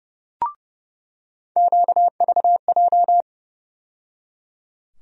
Morse code, heard as a keyed tone, about 30 wpm, 700 Hz. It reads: QVJ
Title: USA Call Sign Postfixes - 3 Letters 30wpm